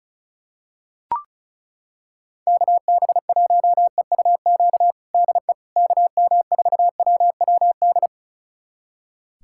Morse code, heard as a keyed tone, about 34 wpm, 690 Hz. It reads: KB1EUQ DE KM4WWD
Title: USA Callsign DE Callsign - 6 Characters - 35wpm